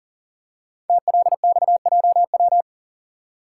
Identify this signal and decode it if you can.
Morse code, heard as a keyed tone, about 39 wpm, 700 Hz. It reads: TLXJW
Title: Code Groups - 5 Letters - 40wpm